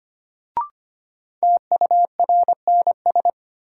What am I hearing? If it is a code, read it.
Morse code, heard as a keyed tone, about 25 wpm, 700 Hz. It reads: TURNS
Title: Sets of 5 Words Encore - New Word Drill - 25wpm